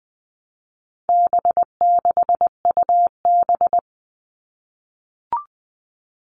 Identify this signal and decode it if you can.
Morse code, heard as a keyed tone, about 19 wpm, 700 Hz. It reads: B6UB
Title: All Cut Numbers - 4-Digit 20wpm